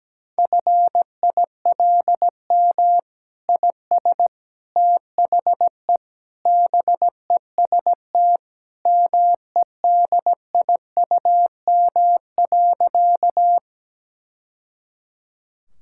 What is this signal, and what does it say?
Morse code, heard as a keyed tone, about 17 wpm, 700 Hz. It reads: FILM IS THE BEST MEDIUM.
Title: Sentences from Top 1000 Words - Review Part 6 - 17wpm